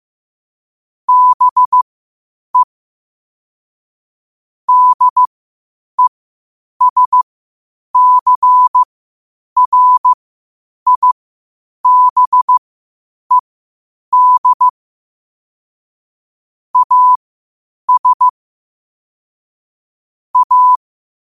Morse code, heard as a keyed tone, about 15 wpm, spaced as for 8 wpm, 1000 Hz. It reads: BE DESCRIBED AS A
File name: qotdfiles / 8wpm